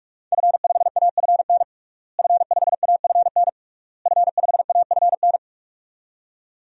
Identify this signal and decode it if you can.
Morse code, heard as a keyed tone, about 42 wpm, 700 Hz. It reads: F5AFN F5AFN F5AFN
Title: DXCC-100 Call Signs - 5 Characters - Repeated 3x - 45wpm